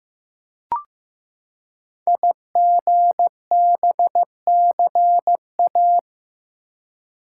Morse code, heard as a keyed tone, about 15 wpm, 700 Hz. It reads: IGBCA
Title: Code Groups - 5 Letters - 15wpm